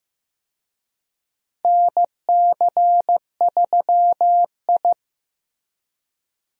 Morse code, heard as a keyed tone, about 15 wpm, 700 Hz. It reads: NC3I
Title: USA Call Signs - 4 Characters - Repeated 2x - 15wpm